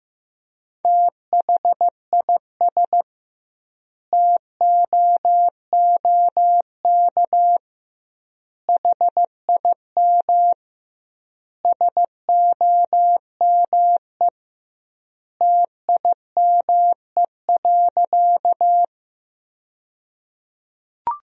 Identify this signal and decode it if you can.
Morse code, heard as a keyed tone, about 15 wpm, 700 Hz. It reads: THIS TOOK HIM SOME TIME.
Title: 2x Word-Spacing - Sentences from Top 300 Words - 15wpm